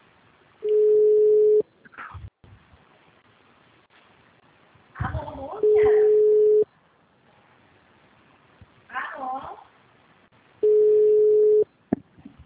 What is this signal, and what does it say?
Ringback tone (Europe, 425 Hz), with speech.